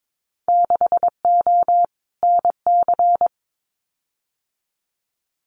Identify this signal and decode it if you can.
Morse code, heard as a keyed tone, about 22 wpm, 700 Hz. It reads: 6O NC
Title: Winter Field Day Exchanges 22wpm